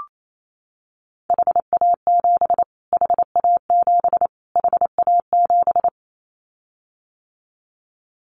Morse code, heard as a keyed tone, about 28 wpm, 700 Hz. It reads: HA7 HA7 HA7
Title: Callsign Prefixes for Top 25 DXCC - Repeated 3x - 28wpm